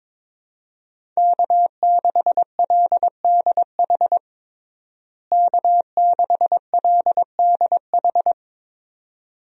Morse code, heard as a keyed tone, about 22 wpm, 700 Hz. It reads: K6LDH K6LDH
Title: USA Call Signs - 5 Characters - Repeated 2x - 22wpm